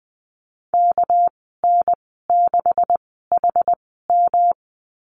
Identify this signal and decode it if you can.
Morse code, heard as a keyed tone, about 20 wpm, spaced as for 15 wpm, 700 Hz. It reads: KN6HM
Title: 213 Lesson 34g - Number 8 - Call Signs 20wpm